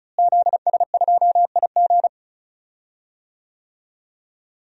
Morse code, heard as a keyed tone, about 35 wpm, 700 Hz. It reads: ZS2IG
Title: DXCC-100 Call Signs - 5 Characters - Repeated 3x - 35wpm